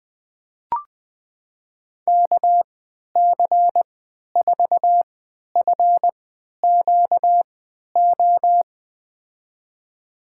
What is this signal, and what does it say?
Morse code, heard as a keyed tone, about 21 wpm, spaced as for 12 wpm, 700 Hz. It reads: KC4FQO
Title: USA Call Signs - 6 Characters - 3x Character Spacing - 20wpm